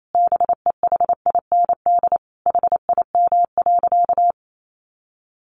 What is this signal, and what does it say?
Morse code, heard as a keyed tone, about 28 wpm, 700 Hz. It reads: BEHIND HIM.
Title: Sentences from Top 700 Words 28wpm